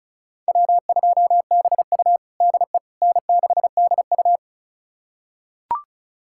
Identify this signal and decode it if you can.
Morse code, heard as a keyed tone, about 36 wpm, 700 Hz. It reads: W2BU DE N6DU
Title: USA Callsign DE Callsign - 4 Characters - 35wpm